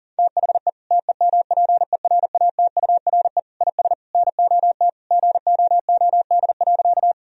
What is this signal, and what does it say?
Morse code, heard as a keyed tone, about 40 wpm, 700 Hz. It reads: THE TEMPERATURE IS NOT GOOD.
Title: Sentences from Top 1000 Words - Review Part 4 - 40wpm